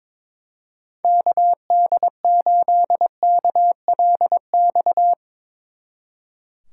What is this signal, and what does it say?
Morse code, heard as a keyed tone, about 22 wpm, 700 Hz. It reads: KD8KLX
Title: Speed Racing - USA 2x3 Call Signs 22wpm